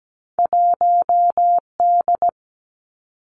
Morse code, heard as a keyed tone, about 17 wpm, 700 Hz. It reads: 1D
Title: Two Challenging Letter-Number Combinations 17wpm